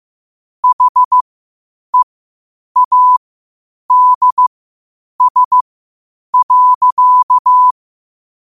Morse code, heard as a keyed tone, about 15 wpm, spaced as for 8 wpm, 1000 Hz. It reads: HEADS.
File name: qotdfiles / 8wpm